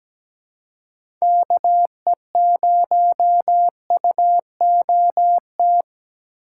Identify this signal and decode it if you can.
Morse code, heard as a keyed tone, about 17 wpm, 700 Hz. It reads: KE0UOT
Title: USA Call Signs - 6 Characters - 2x Character Spacing - 17wpm